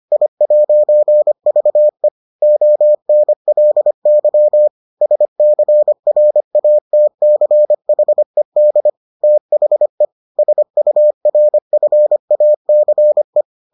Morse code, heard as a keyed tone, about 25 wpm, 600 Hz. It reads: I'VE ONLY SCRATCHED THE SURFACE